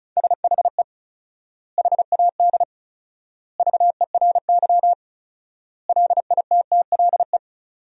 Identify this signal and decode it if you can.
Morse code, heard as a keyed tone, about 36 wpm, 700 Hz. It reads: SHE HAD VERY LITTLE
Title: 4x Word-Spacing - Sentences from Top 400 Words - 35wpm